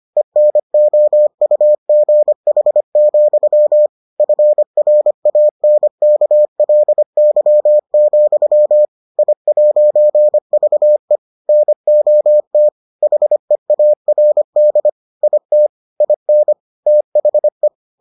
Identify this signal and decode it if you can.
Morse code, heard as a keyed tone, about 25 wpm, 600 Hz. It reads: ENOUGH, FRANKLY, I'VE NOT HEARD IT IN THE